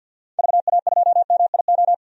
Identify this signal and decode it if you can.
Morse code, heard as a keyed tone, about 50 wpm, 700 Hz. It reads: VA2MIO
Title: DXCC-100 Call Signs - 6 Characters - 50wpm